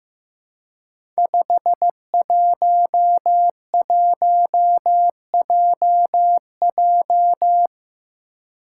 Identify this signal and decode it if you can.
Morse code, heard as a keyed tone, about 15 wpm, 700 Hz. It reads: 511JJ
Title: Five Challenging Letter-Number Combinations 15wpm